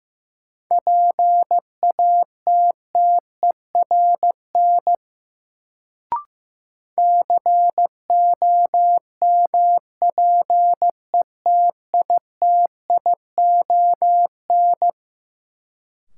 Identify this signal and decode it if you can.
Morse code, heard as a keyed tone, about 15 wpm, 700 Hz. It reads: PATTERN COMPETITION
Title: Sentences from Top 1000 Words - New Word Drill - 15wpm